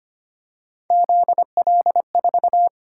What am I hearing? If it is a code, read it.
Morse code, heard as a keyed tone, about 25 wpm, 700 Hz. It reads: ZL4
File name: Three Letter-Number Combinations - 25wpm